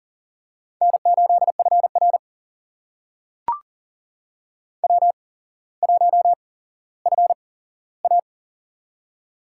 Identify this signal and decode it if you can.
Morse code, heard as a keyed tone, about 40 wpm, spaced as for 17 wpm, 700 Hz. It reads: N8FR W1FA
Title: USA Call Signs - 4 Characters - 8x Character Spacing - 40wpm